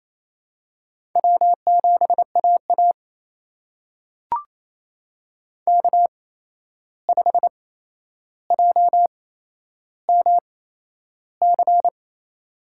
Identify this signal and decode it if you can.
Morse code, heard as a keyed tone, about 28 wpm, spaced as for 11 wpm, 700 Hz. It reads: W7AA K5JMC
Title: USA Call Signs - 8x Character Spacing - 28wpm